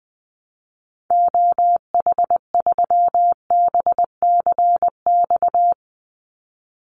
Morse code, heard as a keyed tone, about 20 wpm, 700 Hz. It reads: OH3BCX
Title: Call Signs - 20wpm